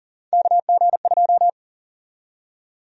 Morse code, heard as a keyed tone, about 40 wpm, 700 Hz. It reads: KG2
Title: USA Call Sign Prefixes - 40wpm